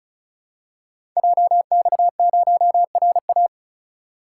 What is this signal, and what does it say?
Morse code, heard as a keyed tone, about 36 wpm, 690 Hz. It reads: JX0RA
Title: Code Groups - 5 Letter:Numbers - 35wpm